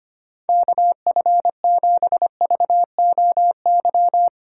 Morse code, heard as a keyed tone, about 25 wpm, 700 Hz. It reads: KF7VOY
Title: USA Call Signs - 6 Characters - 6x Character Spacing - 25wpm